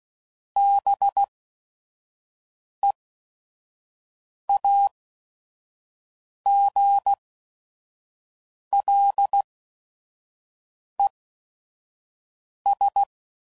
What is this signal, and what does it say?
Morse code, heard as a keyed tone, about 16 wpm, spaced as for 5 wpm, 800 Hz. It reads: BEAGLES